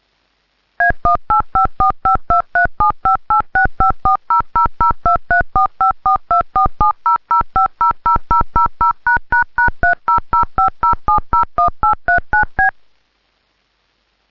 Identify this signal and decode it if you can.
DTMF, heard as telephone touch tones, tones of about 130 ms. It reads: A18545237586540002345424700500000###30050701839B